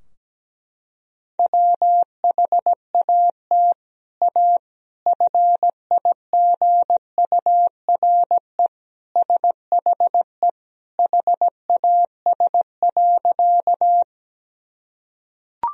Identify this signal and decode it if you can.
Morse code, heard as a keyed tone, about 17 wpm, 700 Hz. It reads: WHAT A FIGURE SHE HAS.
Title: Sentences from Top 1000 Words - Review Part 2 - 17wpm